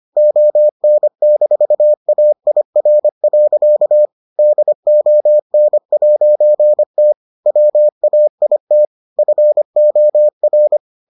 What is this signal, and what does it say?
Morse code, heard as a keyed tone, about 25 wpm, 600 Hz. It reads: ON-AIR. DON'T WAIT FOR